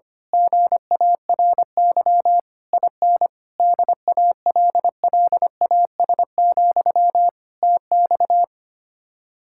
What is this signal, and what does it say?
Morse code, heard as a keyed tone, about 24 wpm, 700 Hz. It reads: GARY IN DALLAS, TX
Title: Speed-Racing Common QSO Phrases Encore - 25wpm